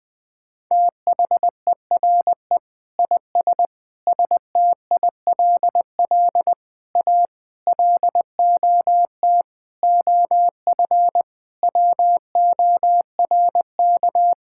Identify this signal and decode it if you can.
Morse code, heard as a keyed tone, about 21 wpm, 700 Hz. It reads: THERE IS STILL A LOT OF WORK
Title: Sentences with 9-Word Phrases - Spoken Beginning - 20wpm